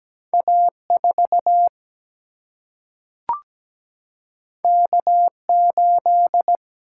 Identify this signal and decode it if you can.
Morse code, heard as a keyed tone, about 17 wpm, 700 Hz. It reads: A4 K8
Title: Single-Letter Single-Number 17wpm